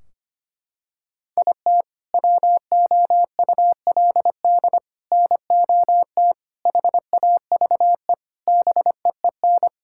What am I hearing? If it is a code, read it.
Morse code, heard as a keyed tone, about 25 wpm, 700 Hz. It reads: IT WOULD NOT HAVE BEEN
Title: Sentences from Top 1600 Words 25wpm